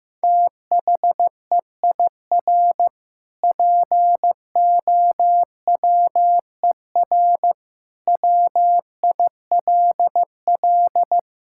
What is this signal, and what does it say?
Morse code, heard as a keyed tone, about 15 wpm, 700 Hz. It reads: THEIR POWER WILL
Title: Sentences from Top 1000 Words - Review Part 6 - 15wpm